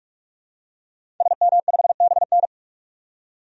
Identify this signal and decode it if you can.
Morse code, heard as a keyed tone, about 42 wpm, 700 Hz. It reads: SM5BN
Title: DXCC-100 Call Signs - 5 Characters - 45wpm